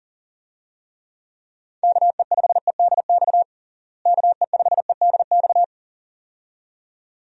Morse code, heard as a keyed tone, about 37 wpm, 700 Hz. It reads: KE5EDX KE5EDX
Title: USA Call Signs - 6 Characters - Repeated 2x - 40wpm